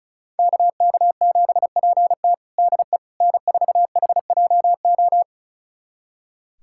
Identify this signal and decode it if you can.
Morse code, heard as a keyed tone, about 36 wpm, 700 Hz. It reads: KK7PT DE N4HJO
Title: USA Callsign DE Callsign - 5 Characters - 35wpm